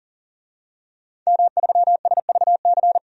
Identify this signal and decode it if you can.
Morse code, heard as a keyed tone, about 37 wpm, 700 Hz. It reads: M3SVC
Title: DXCC-100 Call Signs - 5 Characters - 40wpm